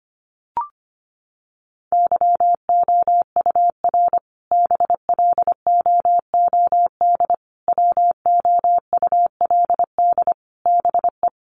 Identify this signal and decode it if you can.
Morse code, heard as a keyed tone, about 25 wpm, 700 Hz. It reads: YOUR BLOOD WOULD BE
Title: Sentences from Top 1100 Words 25wpm